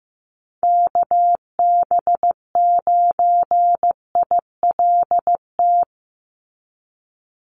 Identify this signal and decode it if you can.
Morse code, heard as a keyed tone, about 15 wpm, 700 Hz. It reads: KB9ILT
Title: NA SOTA Chaser Callsigns 15wpm